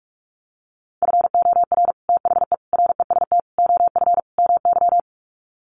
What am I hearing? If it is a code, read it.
Morse code, heard as a keyed tone, about 44 wpm, 700 Hz. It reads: FOR THE REST OF MY
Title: Sets of 5 Words 45wpm